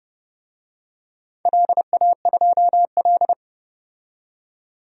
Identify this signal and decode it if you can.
Morse code, heard as a keyed tone, about 30 wpm, 700 Hz. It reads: LA2L
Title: DXCC-100 Call Signs - 4 Characters - 30wpm